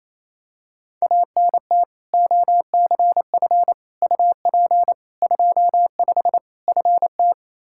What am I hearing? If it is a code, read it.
Morse code, heard as a keyed tone, about 28 wpm, 700 Hz. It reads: ANT OCF UP 25 FT
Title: Practice QSO - Exchange 2 - Info - 28wpm